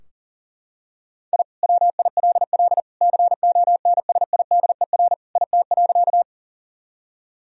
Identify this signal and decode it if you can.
Morse code, heard as a keyed tone, about 40 wpm, 700 Hz. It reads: I WILL CONSIDER IT.
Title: Sentences from Top 1000 Words - Review Part 4 - 40wpm